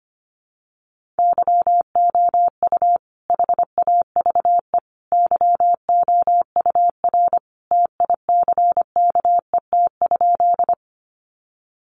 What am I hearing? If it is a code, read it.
Morse code, heard as a keyed tone, about 25 wpm, 700 Hz. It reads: YOU HAVE YOUR TICKET?
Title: Sentences from Top 1400 Words 25wpm